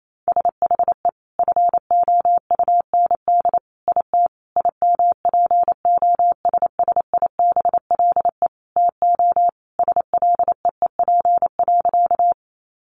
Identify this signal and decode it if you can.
Morse code, heard as a keyed tone, about 28 wpm, 700 Hz. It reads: SHE FOUND IT IMPOSSIBLE TO SLEEP.
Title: Sentences from Top 1100 Words 28wpm